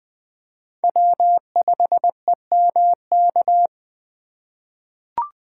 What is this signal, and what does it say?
Morse code, heard as a keyed tone, about 21 wpm, 700 Hz. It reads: W5EMK
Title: USA Call Signs - 5 Characters - 4x Character Spacing - 20wpm